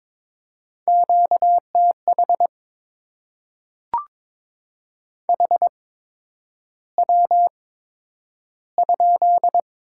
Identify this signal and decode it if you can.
Morse code, heard as a keyed tone, about 22 wpm, spaced as for 9 wpm, 700 Hz. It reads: QTH HW?
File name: Top 20 QSO Elements - 8x Character Spacing - 22wpm